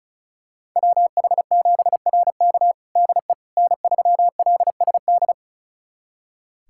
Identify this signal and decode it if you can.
Morse code, heard as a keyed tone, about 34 wpm, 700 Hz. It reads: WH7RK DE N3LSD
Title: USA Callsign DE Callsign - 5 Characters - 35wpm